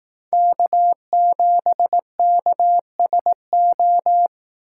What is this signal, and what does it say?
Morse code, heard as a keyed tone, about 16 wpm, 700 Hz. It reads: K7KSO
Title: Speed Racing - USA 1x3 Call Signs 15wpm